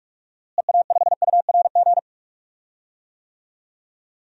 Morse code, heard as a keyed tone, about 46 wpm, 700 Hz. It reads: EA5URG
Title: DXCC-100 Call Signs - 6 Characters - Repeated 3x - 45wpm